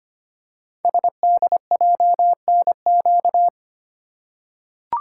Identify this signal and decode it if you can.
Morse code, heard as a keyed tone, about 25 wpm, 700 Hz. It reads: SDJNQ
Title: Code Groups - 5 Letter:Numbers - 25wpm